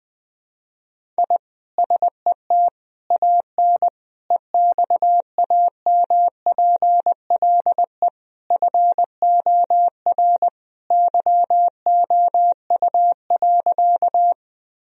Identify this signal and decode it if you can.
Morse code, heard as a keyed tone, about 20 wpm, 700 Hz. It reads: I SET AN EXAMPLE FOR YOU.
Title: Sentences from Top 1000 Words - Review Part 1 - 20wpm